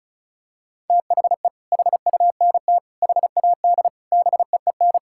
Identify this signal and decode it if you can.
Morse code, heard as a keyed tone, about 34 wpm, 700 Hz. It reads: THE HUNT HAD BEEN
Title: Sentences from Top 1600 Words 35wpm